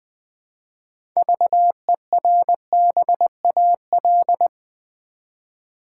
Morse code, heard as a keyed tone, about 20 wpm, 700 Hz. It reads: VERBAL
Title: 6 Letter Words - First 3 Letters Given - 20wpm